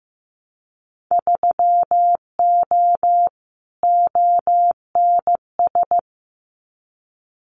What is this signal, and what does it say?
Morse code, heard as a keyed tone, about 15 wpm, 700 Hz. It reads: 3O ONS
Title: Winter Field Day Exchanges 15wpm